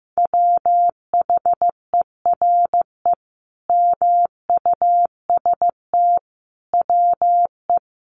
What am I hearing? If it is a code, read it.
Morse code, heard as a keyed tone, about 15 wpm, 700 Hz. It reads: WHERE MUST WE
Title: Sentences from Top 1300 Words 15wpm